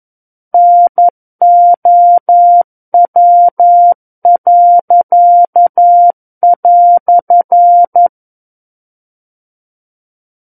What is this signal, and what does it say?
Morse code, heard as a keyed tone, about 11 wpm, 700 Hz. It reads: NOW."
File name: 11 WPM